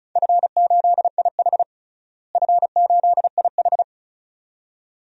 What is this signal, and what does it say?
Morse code, heard as a keyed tone, about 36 wpm, 700 Hz. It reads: F8IH F8IH
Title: DXCC-100 Call Signs - 4 Characters - Repeated 2x - 35wpm